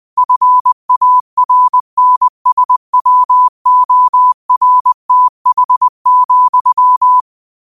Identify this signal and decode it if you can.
Morse code, heard as a keyed tone, about 20 wpm, 1000 Hz. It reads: FARNSWORTH,
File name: 20wpm